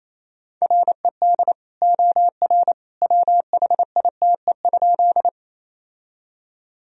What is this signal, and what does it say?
Morse code, heard as a keyed tone, about 28 wpm, 700 Hz. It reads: RED OR WHITE?
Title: Sentences from Top 1000 Words - Review Part 2 - 28wpm